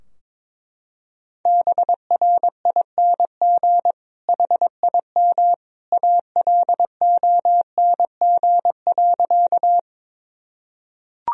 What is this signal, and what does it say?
Morse code, heard as a keyed tone, about 22 wpm, 700 Hz. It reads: BRING HIM ALONG.
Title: Sentences from Top 1000 Words - Review Part 3 - 22wpm